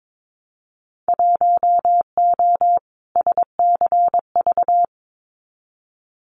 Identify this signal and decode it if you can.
Morse code, heard as a keyed tone, about 22 wpm, 700 Hz. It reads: 1O SCV
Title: Winter Field Day Exchanges 22wpm